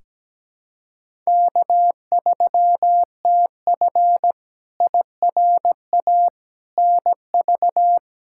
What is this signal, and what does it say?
Morse code, heard as a keyed tone, about 17 wpm, 700 Hz. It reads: K3TF IRA NV
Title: K1USN SST QSO Practice - 17wpm